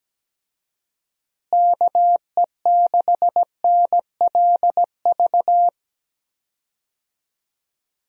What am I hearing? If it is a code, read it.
Morse code, heard as a keyed tone, about 17 wpm, 700 Hz. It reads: KE6NLV
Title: USA Call Signs - 6 Characters - Repeated 3x - 17wpm